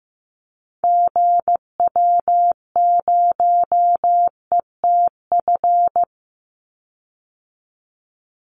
Morse code, heard as a keyed tone, about 15 wpm, 700 Hz. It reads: GW0ETF
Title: Call Signs Repeated 3x - 15wpm